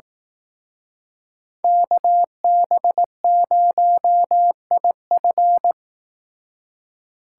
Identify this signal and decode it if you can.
Morse code, heard as a keyed tone, about 18 wpm, 700 Hz. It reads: KB0IF
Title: Speed Racing - USA 2x2 Call Signs 15wpm